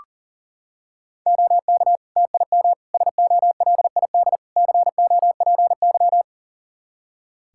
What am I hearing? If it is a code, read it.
Morse code, heard as a keyed tone, about 40 wpm, 700 Hz. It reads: OK TIM SOLID COPY
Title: Practice QSO - Exchange 3 - More Info - 40wpm